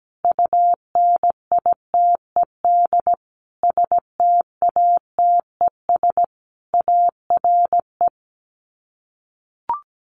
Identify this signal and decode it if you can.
Morse code, heard as a keyed tone, about 17 wpm, 700 Hz. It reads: UNITED STATES ARE
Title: Sets of 5 Words Encore 17wpm